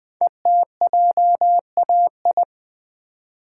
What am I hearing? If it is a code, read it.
Morse code, heard as a keyed tone, about 20 wpm, 700 Hz. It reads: ETJAI
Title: Code Groups - 5 Letters - 20wpm